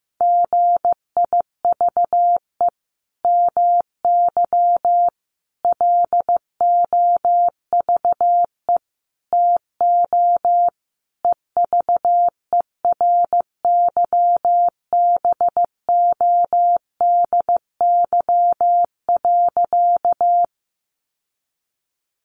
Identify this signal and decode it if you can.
Morse code, heard as a keyed tone, about 15 wpm, 700 Hz. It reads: GIVE MY LOVE TO EVERYBODY.
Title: Sentences from Top 1400 Words 15wpm